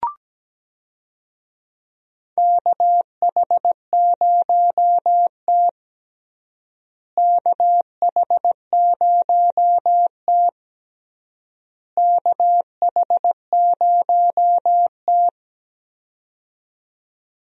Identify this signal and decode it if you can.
Morse code, heard as a keyed tone, about 17 wpm, 700 Hz. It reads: KH0T KH0T KH0T
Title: USA Call Signs - 4 Characters - Repeated 3x - 17wpm